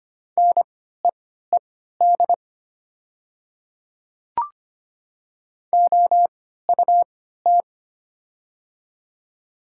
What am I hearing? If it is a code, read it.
Morse code, heard as a keyed tone, about 25 wpm, spaced as for 14 wpm, 700 Hz. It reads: NEED OUT
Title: 3x Character Spacing - Top 300 Words - 25wpm